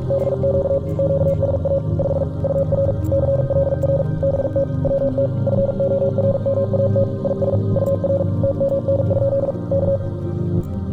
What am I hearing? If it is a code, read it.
Morse code, heard as a keyed tone, about 45 wpm, 550 Hz. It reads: BQ QSA 5 UR CLR BT WT FORMAT IS UR TMN? K